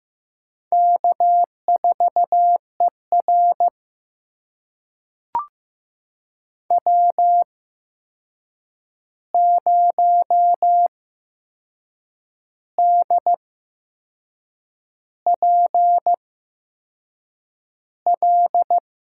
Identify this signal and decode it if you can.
Morse code, heard as a keyed tone, about 15 wpm, spaced as for 6 wpm, 700 Hz. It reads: K4ER W0DPL
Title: USA Call Signs - 8x Character Spacing - 15wpm